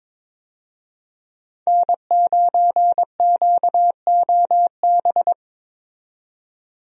Morse code, heard as a keyed tone, about 22 wpm, 700 Hz. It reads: N9QOB
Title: USA Call Signs - 5 Characters - 22wpm